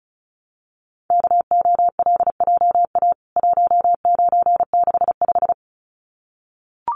Morse code, heard as a keyed tone, about 34 wpm, 700 Hz. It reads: KOLJA 1965
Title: CWT Exchange 35wpm